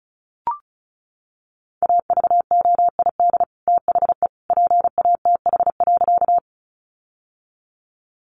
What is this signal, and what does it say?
Morse code, heard as a keyed tone, about 34 wpm, 700 Hz. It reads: AVOID THE PATH.
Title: Sentences from Top 1400 Words 35wpm